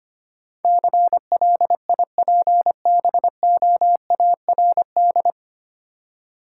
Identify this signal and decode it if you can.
Morse code, heard as a keyed tone, about 25 wpm, 690 Hz. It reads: CLIPBOARD
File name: practice / 25wpm / Easy Words with Hints - 25wpm